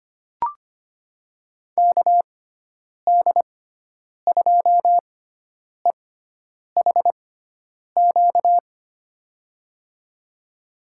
Morse code, heard as a keyed tone, about 25 wpm, spaced as for 9 wpm, 700 Hz. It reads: KD2EHQ
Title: USA Call Signs - 6 Characters - 6x Character Spacing - 25wpm